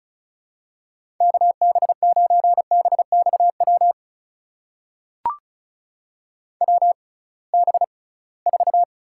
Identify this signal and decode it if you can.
Morse code, heard as a keyed tone, about 36 wpm, spaced as for 23 wpm, 690 Hz. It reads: KB9BXW WB4
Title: USA Call Signs - 6 Characters - 6x Character Spacing - 35wpm